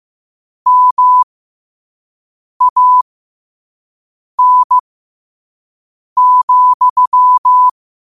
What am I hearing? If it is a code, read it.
Morse code, heard as a keyed tone, about 15 wpm, spaced as for 5 wpm, 1000 Hz. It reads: MAN,
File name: qotdfiles / 5wpm